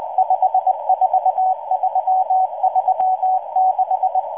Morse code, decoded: 54336